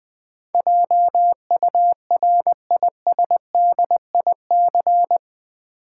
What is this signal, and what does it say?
Morse code, heard as a keyed tone, about 20 wpm, 700 Hz. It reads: JURISDIC
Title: Build A Word - 4 Letters At A Time - 12-Letter Words - 20wpm